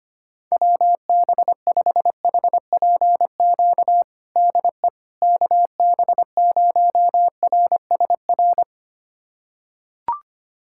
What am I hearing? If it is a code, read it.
Morse code, heard as a keyed tone, about 25 wpm, 700 Hz. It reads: WB5HPQ DE KB0RSR
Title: USA Callsign DE Callsign - 6 Characters - 25wpm